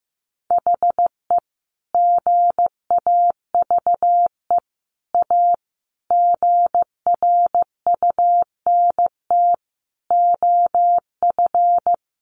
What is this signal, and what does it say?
Morse code, heard as a keyed tone, about 15 wpm, 700 Hz. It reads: HE GAVE A GRUNT OF
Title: A Sound Of Thunder 15wpm